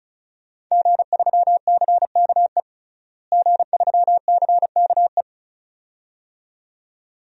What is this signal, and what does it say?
Morse code, heard as a keyed tone, about 36 wpm, 700 Hz. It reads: G3CKE G3CKE
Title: DXCC-100 Call Signs - 5 Characters - Repeated 2x - 35wpm